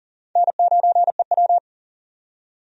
Morse code, heard as a keyed tone, about 40 wpm, 700 Hz. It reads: N9EW
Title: USA Call Signs - 4 Characters - 40wpm